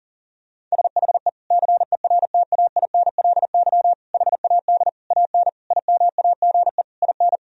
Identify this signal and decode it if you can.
Morse code, heard as a keyed tone, about 40 wpm, 700 Hz. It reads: SHE CERTAINLY HAD AN IMAGE IN
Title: Sentences from Top 1000 Words - Review Part 4 - 40wpm